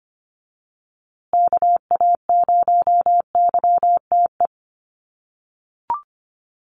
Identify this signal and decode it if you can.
Morse code, heard as a keyed tone, about 25 wpm, 700 Hz. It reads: KA0YTE
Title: USA Call Signs Repeated 2x 25wpm